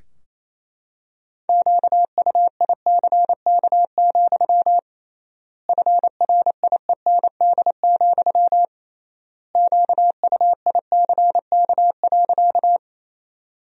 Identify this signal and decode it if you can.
Morse code, heard as a keyed tone, about 28 wpm, 700 Hz. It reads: QUICK, FRIEND, QUICK.
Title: 3x Word-Spacing - Sentences from Top 500 Words - 28wpm